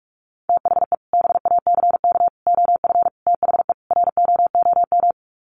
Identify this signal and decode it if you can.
Morse code, heard as a keyed tone, about 44 wpm, 700 Hz. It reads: THE BACK OF THE ROOM
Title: Sets of 5 Words Encore 45wpm